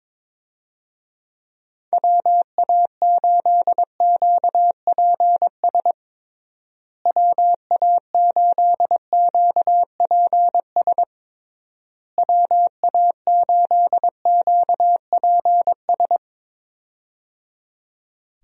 Morse code, heard as a keyed tone, about 22 wpm, 700 Hz. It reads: WA8QPS WA8QPS WA8QPS
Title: USA Call Signs - 6 Characters - Repeated 3x - 22wpm